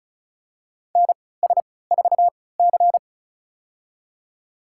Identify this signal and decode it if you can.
Morse code, heard as a keyed tone, about 34 wpm, spaced as for 20 wpm, 700 Hz. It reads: NS4C
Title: USA Call Signs - 3x Character Spacing - 35wpm